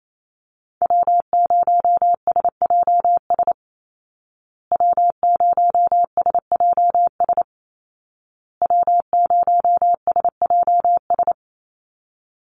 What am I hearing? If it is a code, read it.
Morse code, heard as a keyed tone, about 28 wpm, 700 Hz. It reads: W0SJS W0SJS W0SJS
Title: Call Signs Repeated 2x - 28wpm